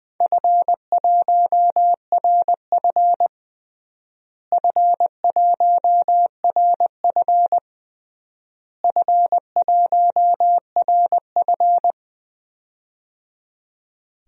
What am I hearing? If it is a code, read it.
Morse code, heard as a keyed tone, about 21 wpm, 700 Hz. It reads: F1RF F1RF F1RF
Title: DXCC-100 Call Signs - 4 Characters - Repeated 3x - 20wpm